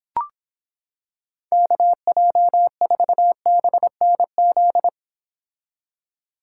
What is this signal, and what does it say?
Morse code, heard as a keyed tone, about 26 wpm, 700 Hz. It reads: KJ4BNZ